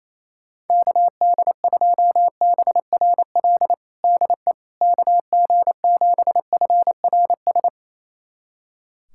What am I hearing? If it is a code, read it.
Morse code, heard as a keyed tone, about 28 wpm, 700 Hz. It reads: KD2BRL DE KG7FRS
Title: USA Callsign DE Callsign 28wpm